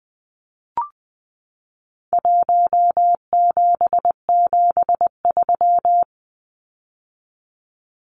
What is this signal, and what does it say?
Morse code, heard as a keyed tone, about 20 wpm, 700 Hz. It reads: 1773
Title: Numbers - Contest Style - 4-Digit 20wpm